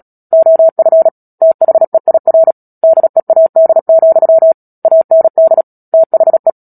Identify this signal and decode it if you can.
Morse code, heard as a keyed tone, about 37 wpm, 650 Hz. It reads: OF THEIR DEAD, AND THE
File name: Book Two / 10-Chapter 10 - Epilogue - 36 WPM